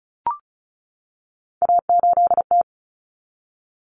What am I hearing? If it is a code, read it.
Morse code, heard as a keyed tone, about 34 wpm, 700 Hz. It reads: A8T
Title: Common Cut Numbers - 3-Digit 35wpm